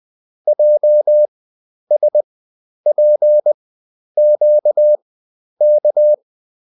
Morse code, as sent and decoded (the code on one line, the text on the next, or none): .--- ... .--. --.- -.-
JSPQK